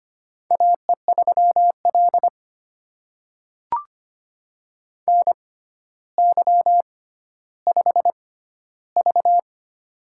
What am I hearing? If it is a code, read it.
Morse code, heard as a keyed tone, about 25 wpm, spaced as for 13 wpm, 700 Hz. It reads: AE3L NY5V
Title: USA Call Signs - 4 Characters - 6x Character Spacing - 25wpm